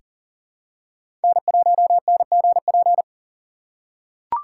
Morse code, heard as a keyed tone, about 40 wpm, 700 Hz. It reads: N1NGP
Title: USA Call Signs - 4x Character Spacing - 40wpm